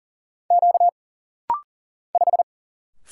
Morse code, heard as a keyed tone, about 40 wpm, 700 Hz. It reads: Q 5
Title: Single Letter-Number Alt - Rapid-Fire 40wpm